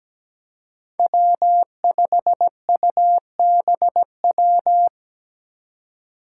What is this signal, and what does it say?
Morse code, heard as a keyed tone, about 17 wpm, 700 Hz. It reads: W5UBW
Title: Five Challenging Letter-Number Combinations 17wpm